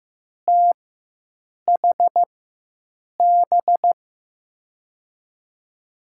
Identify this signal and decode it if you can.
Morse code, heard as a keyed tone, about 15 wpm, spaced as for 7 wpm, 700 Hz. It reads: THB
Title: USA Call Signs - 6 Characters - 4x Character Spacing - 15wpm